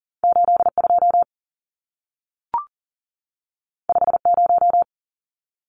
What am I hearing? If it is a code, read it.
Morse code, heard as a keyed tone, about 40 wpm, 700 Hz. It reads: 82 50
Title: Numbers - 2-Digit 40wpm